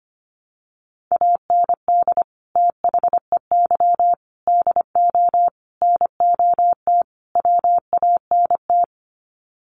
Morse code, heard as a keyed tone, about 25 wpm, 700 Hz. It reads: AND THEY DO NOT WANT
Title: Sets of 5 Words Encore2 25wpm